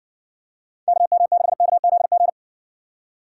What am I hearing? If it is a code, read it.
Morse code, heard as a keyed tone, about 57 wpm, 700 Hz. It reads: KM6KZG